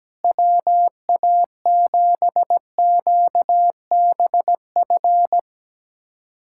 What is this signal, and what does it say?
Morse code, heard as a keyed tone, about 17 wpm, 700 Hz. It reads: WA7QBF
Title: USA Callsign DE Callsign 17wpm